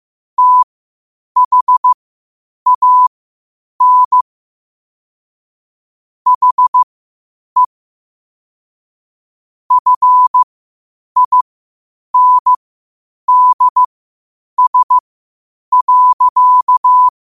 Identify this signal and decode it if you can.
Morse code, heard as a keyed tone, about 15 wpm, spaced as for 8 wpm, 1000 Hz. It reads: THAN HE FINDS.
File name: qotdfiles / 8wpm